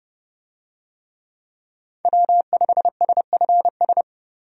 Morse code, heard as a keyed tone, about 30 wpm, 700 Hz. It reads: W5SFS
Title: USA Call Signs - 5 Characters - 30wpm